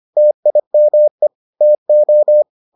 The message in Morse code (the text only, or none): TIME TO